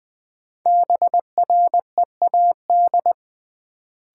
Morse code, heard as a keyed tone, about 20 wpm, 700 Hz. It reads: BREAD
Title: Sentences from Top 1600 Words - New Word Drill - 20wpm